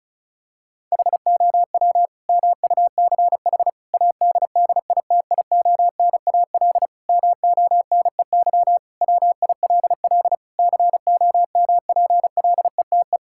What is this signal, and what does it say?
Morse code, heard as a keyed tone, about 36 wpm, 700 Hz. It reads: HOW MUCH ADDITIONAL MONEY WILL COMPLETE
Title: Sentences from Top 1000 Words - Review Part 5 - 35wpm